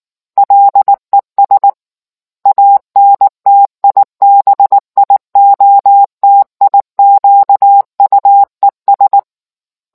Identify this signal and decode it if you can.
Morse code, heard as a keyed tone, about 19 wpm, 800 Hz. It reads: LES ANTIBIOTIQUES